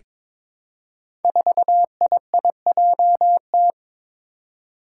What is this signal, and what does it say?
Morse code, heard as a keyed tone, about 21 wpm, 700 Hz. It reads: 4IIJT
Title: Five Challenging Letter-Number Combinations 22wpm